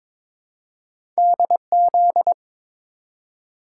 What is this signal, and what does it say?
Morse code, heard as a keyed tone, about 22 wpm, 700 Hz. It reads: DZ
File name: practice / 22wpm / Two Letter-Number Combinations - 22wpm